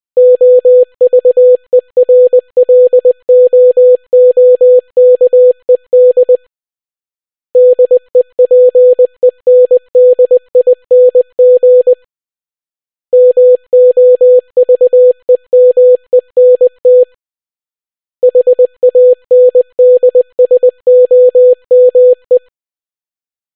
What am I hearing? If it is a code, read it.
Morse code, heard as a keyed tone, about 20 wpm, 500 Hz. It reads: OVERLOOKED DEPENDING MOVEMENT HANDSOME